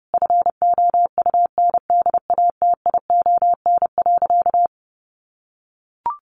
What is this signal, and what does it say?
Morse code, heard as a keyed tone, about 30 wpm, 700 Hz. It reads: FOUNDATION.
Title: Sentences from Top 1200 Words 30wpm